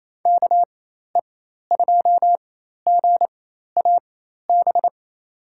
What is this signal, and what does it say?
Morse code, heard as a keyed tone, about 28 wpm, spaced as for 13 wpm, 700 Hz. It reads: KE2GAB
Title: USA Call Signs - 4x Character Spacing - 28wpm